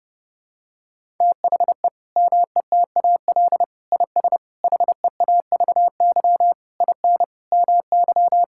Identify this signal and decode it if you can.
Morse code, heard as a keyed tone, about 30 wpm, 700 Hz. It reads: THE METAL IS HEAVY IN MY
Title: Sentences from Top 1000 Words - Review Part 6 - 30wpm